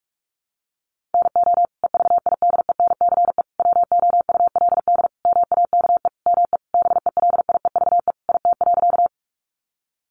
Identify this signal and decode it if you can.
Morse code, heard as a keyed tone, about 44 wpm, 700 Hz. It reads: NO EVIDENCE WOULD MAKE ME BELIEVE IT.
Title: Sentences from Top 1300 Words 45wpm